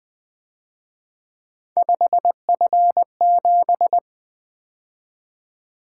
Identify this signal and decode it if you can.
Morse code, heard as a keyed tone, about 23 wpm, 700 Hz. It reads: 5F7
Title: Speed-Racing - Three Letter-Number Combinations - 20wpm